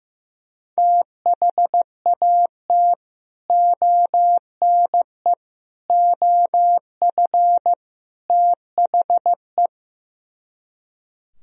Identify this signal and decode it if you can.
Morse code, heard as a keyed tone, about 15 wpm, 700 Hz. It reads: THAT ONE OF THE
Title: Speed-Racing - Sets of 4 Words 15wpm